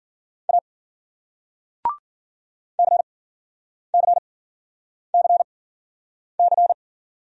Morse code, heard as a keyed tone, about 54 wpm, spaced as for 9 wpm, 700 Hz. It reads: A CCCC